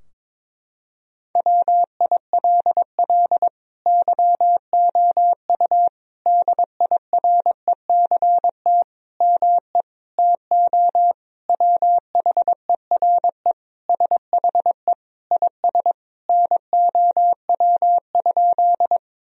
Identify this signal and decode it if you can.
Morse code, heard as a keyed tone, about 22 wpm, 700 Hz. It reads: WILL YOU DIRECT ME TO WHERE SHE IS NOW?
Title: Sentences from Top 1000 Words - Review Part 2 - 22wpm